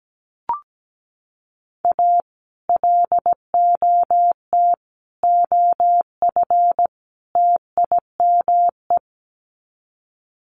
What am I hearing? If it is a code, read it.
Morse code, heard as a keyed tone, about 17 wpm, 700 Hz. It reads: A LOT OF TIME
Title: Sets of 4 Words 17wpm